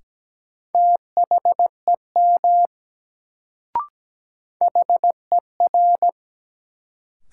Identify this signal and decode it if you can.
Morse code, heard as a keyed tone, about 17 wpm, 700 Hz. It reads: THEM HER
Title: Sets of 2 Words - New Word Drill - 17wpm